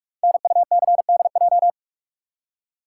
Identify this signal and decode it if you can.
Morse code, heard as a keyed tone, about 44 wpm, 700 Hz. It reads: NUCDJ
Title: Code Groups - 5 Letters - 45wpm